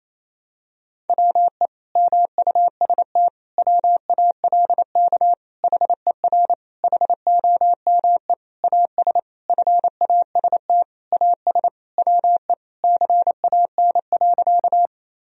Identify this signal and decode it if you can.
Morse code, heard as a keyed tone, about 27 wpm, 700 Hz. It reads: WE MUST WALK HER HOME AS FAST AS WE CAN.
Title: Sentences from Top 1000 Words - Review Part 2 - 28wpm